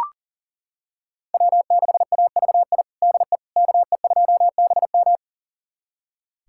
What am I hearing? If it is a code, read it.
Morse code, heard as a keyed tone, about 40 wpm, 700 Hz. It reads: W6AVI DE KE2BM
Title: USA Callsign DE Callsign - 5 Characters - 40wpm